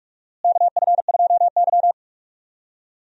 Morse code, heard as a keyed tone, about 43 wpm, 700 Hz. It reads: KF2Y
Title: USA Call Signs - 4 Characters - Repeated 2x - 45wpm